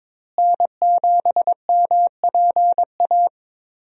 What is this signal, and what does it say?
Morse code, heard as a keyed tone, about 21 wpm, 700 Hz. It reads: N7MPA
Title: USA Call Signs - 5 Characters - Repeated 3x - 22wpm